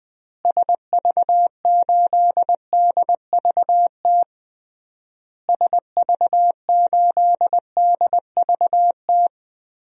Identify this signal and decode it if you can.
Morse code, heard as a keyed tone, about 21 wpm, 700 Hz. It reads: SV8DVT SV8DVT
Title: DXCC-100 Call Signs - 6 Characters - Repeated 3x - 20wpm